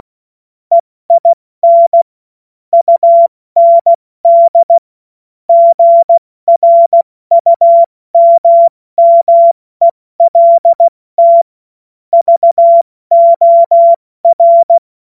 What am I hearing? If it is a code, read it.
Morse code, heard as a keyed tone, about 16 wpm, spaced as for 14 wpm, 680 Hz. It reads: EIN UND GRUMMELT VOR
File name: Morsekurs / Witze / Witz 11. Mai 2021, 70 F80